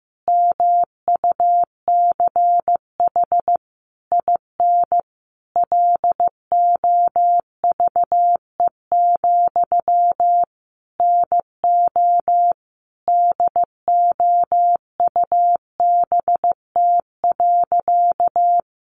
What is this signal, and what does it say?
Morse code, heard as a keyed tone, about 15 wpm, 700 Hz. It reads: MUCH IN LOVE, NO DOUBT.
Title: Sentences from Top 1500 Words 15wpm